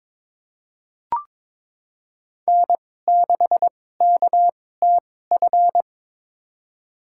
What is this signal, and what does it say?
Morse code, heard as a keyed tone, about 22 wpm, spaced as for 16 wpm, 700 Hz. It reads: N6KTF
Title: USA Call Signs - 5 Characters - 2x Character Spacing - 22wpm